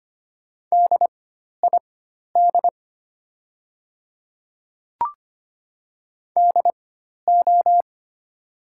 Morse code, heard as a keyed tone, about 25 wpm, spaced as for 12 wpm, 700 Hz. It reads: DID DO
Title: 4x Character Spacing - Top 100 Words - 25wpm